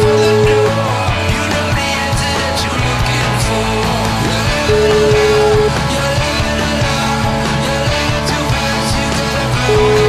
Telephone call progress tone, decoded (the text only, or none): ringback tone